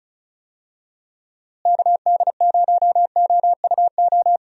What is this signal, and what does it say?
Morse code, heard as a keyed tone, about 34 wpm, 700 Hz. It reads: KD0OUO
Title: USA Call Signs - 6 Characters - Repeated 3x - 35wpm